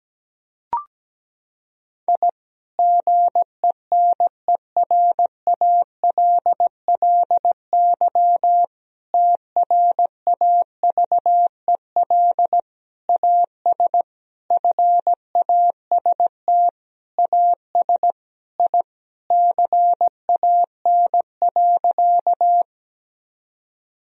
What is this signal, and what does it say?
Morse code, heard as a keyed tone, about 17 wpm, 700 Hz. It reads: I GENERALLY TRAVEL AS FAST AS I CAN.
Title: Sentences from Top 1000 Words - Review Part 4 - 17wpm